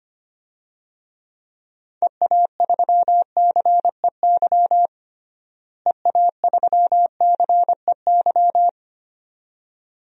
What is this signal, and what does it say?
Morse code, heard as a keyed tone, about 25 wpm, 700 Hz. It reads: EA3CEY EA3CEY
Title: DXCC-100 Call Signs - 6 Characters - Repeated 2x - 25wpm